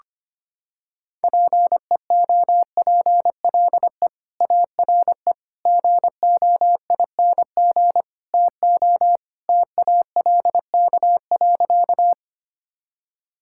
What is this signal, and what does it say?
Morse code, heard as a keyed tone, about 25 wpm, 700 Hz. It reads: PEOPLE ARE GOING TO TALK.
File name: practice / 25wpm / Sentences with 4-Word Phrases - 25wpm